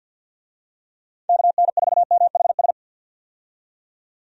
Morse code, heard as a keyed tone, about 50 wpm, 700 Hz. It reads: KN4MHS
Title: USA Call Signs - 6 Characters - 50wpm